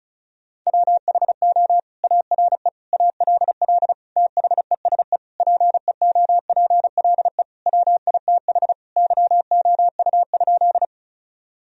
Morse code, heard as a keyed tone, about 34 wpm, 700 Hz. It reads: WHO ARE ALL THESE PEOPLE WITH YOU?
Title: Sentences from Top 1000 Words - Review Part 1 - 35wpm